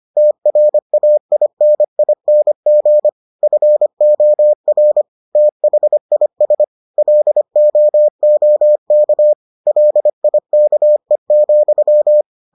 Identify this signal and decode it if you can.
Morse code, heard as a keyed tone, about 25 wpm, 600 Hz. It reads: TRAINING FOR THIS LOOK LIKE,